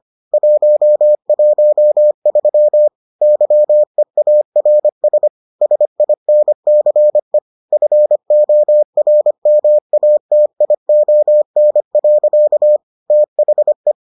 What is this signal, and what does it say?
Morse code, heard as a keyed tone, about 25 wpm, 600 Hz. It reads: 113 YEARS SINCE FORMATION. THE